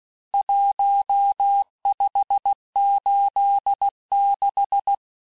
Morse code, heard as a keyed tone, about 16 wpm, 800 Hz. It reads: 1586
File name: cours-du-lundi files / Cours-1586-texte-chiffre-vitesse-15mpm